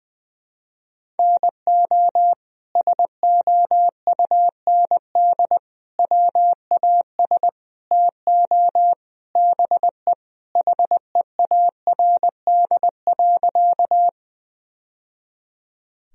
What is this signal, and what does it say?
Morse code, heard as a keyed tone, about 21 wpm, 700 Hz. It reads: NO SOUND WAS TO BE HEARD.
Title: Sentences from Top 1000 Words - Review Part 1 - 20wpm